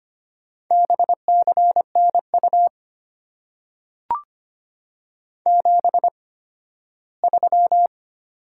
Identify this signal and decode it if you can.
Morse code, heard as a keyed tone, about 25 wpm, spaced as for 15 wpm, 700 Hz. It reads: BCNU 73
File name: Top 90 QSO Elements - 8x Character Spacing - 25wpm